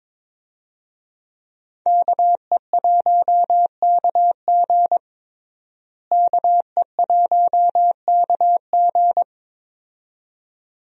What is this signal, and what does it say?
Morse code, heard as a keyed tone, about 22 wpm, 700 Hz. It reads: KE1KG KE1KG
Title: USA Call Signs - 5 Characters - Repeated 2x - 22wpm